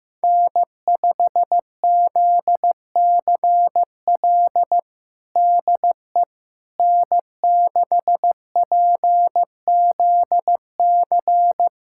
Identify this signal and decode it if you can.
Morse code, heard as a keyed tone, about 15 wpm, 700 Hz. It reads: N5ZCL DE N6PZC
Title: USA Callsign DE Callsign 15wpm